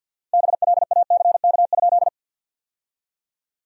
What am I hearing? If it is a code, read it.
Morse code, heard as a keyed tone, about 47 wpm, 700 Hz. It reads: BLACK?
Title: 4x Word-Spacing - Sentences from Top 400 Words - 50wpm